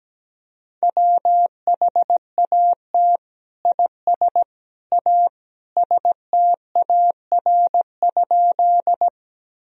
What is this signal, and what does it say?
Morse code, heard as a keyed tone, about 17 wpm, 700 Hz. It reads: WHAT IS A STAR?
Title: Sentences from Top 1000 Words - Review Part 2 - 17wpm